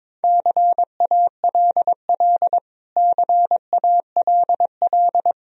Morse code, heard as a keyed tone, about 22 wpm, 700 Hz. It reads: CALL CALL
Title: Top 100 Words Repeated 22wpm